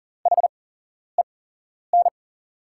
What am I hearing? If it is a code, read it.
Morse code, heard as a keyed tone, about 40 wpm, spaced as for 11 wpm, 690 Hz. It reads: HEN
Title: 8x Character Spacing - Top 100 Words - 40wpm